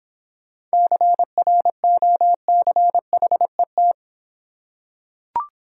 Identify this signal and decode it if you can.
Morse code, heard as a keyed tone, about 26 wpm, 700 Hz. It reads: CROCHET